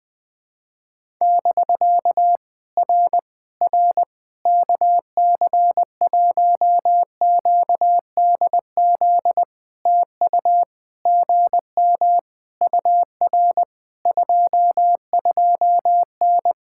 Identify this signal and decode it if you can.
Morse code, heard as a keyed tone, about 20 wpm, 700 Hz. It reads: <BK> R R KC1QDZ TU GM UR 22N
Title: POTA QSO Practice - 20wpm